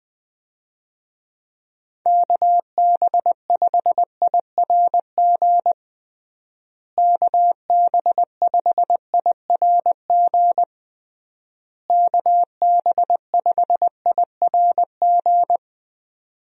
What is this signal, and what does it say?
Morse code, heard as a keyed tone, about 20 wpm, 700 Hz. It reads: KB5IRG KB5IRG KB5IRG
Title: USA Call Signs - 6 Characters - Repeated 3x - 20wpm